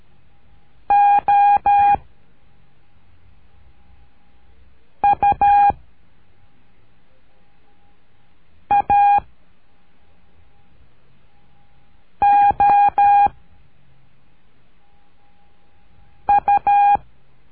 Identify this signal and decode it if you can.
Morse code, heard as a keyed tone, about 13 wpm, 810 Hz. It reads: O U A O U
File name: cursos / CURSO DE TELEGRAFIA / Telegrafia - Trilha 01